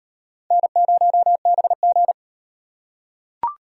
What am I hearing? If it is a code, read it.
Morse code, heard as a keyed tone, about 37 wpm, 700 Hz. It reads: N0BG